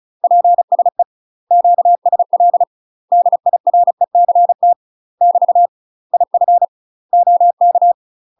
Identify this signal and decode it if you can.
Morse code, heard as a keyed tone, about 36 wpm, 700 Hz. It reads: PSE QSL DIRECT <BT> IF OK
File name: UK MASTER QSOs TRAINING / 35WPM / UK MASTER QSOs 0033